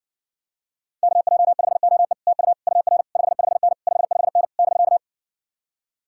Morse code, heard as a keyed tone, about 59 wpm, 700 Hz. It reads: KJ5OE TU UR 55N 55N <BK>